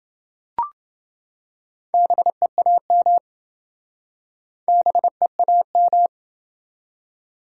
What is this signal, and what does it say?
Morse code, heard as a keyed tone, about 27 wpm, 700 Hz. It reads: BEAM BEAM